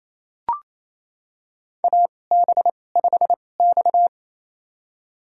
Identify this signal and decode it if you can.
Morse code, heard as a keyed tone, about 28 wpm, spaced as for 20 wpm, 700 Hz. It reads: AB5X
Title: USA Call Signs - 4 Characters - 2x Character Spacing - 28wpm